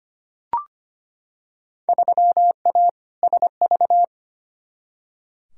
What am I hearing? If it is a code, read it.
Morse code, heard as a keyed tone, about 25 wpm, 700 Hz. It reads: 3A SV
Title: Field Day Exchanges - 25wpm